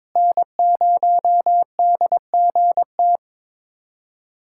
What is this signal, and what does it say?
Morse code, heard as a keyed tone, about 21 wpm, 700 Hz. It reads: N0DGT
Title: USA Callsign DE Callsign - 5 Characters - 22wpm